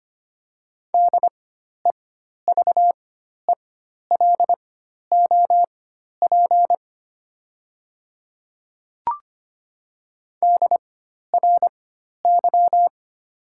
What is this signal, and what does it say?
Morse code, heard as a keyed tone, about 25 wpm, spaced as for 12 wpm, 700 Hz. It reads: DEVELOP DRY
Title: 4x Character Spacing - Top 500 Words - 25wpm